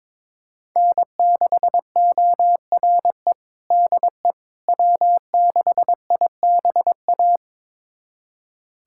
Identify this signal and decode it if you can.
Morse code, heard as a keyed tone, about 22 wpm, 700 Hz. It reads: N6ORE DE W6IBA
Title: USA Callsign DE Callsign - 5 Characters - 22wpm